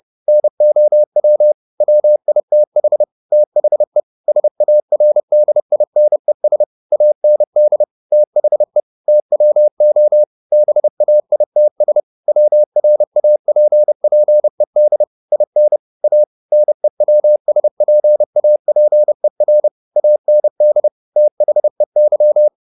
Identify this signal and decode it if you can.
Morse code, heard as a keyed tone, about 30 wpm, 600 Hz. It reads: NOW WITH THE SARDINES AND THE TWO BAITS WRAPPED IN A NEWSPAPER AND THEY
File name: the old man and the sea / the old man and the sea 1 30wpm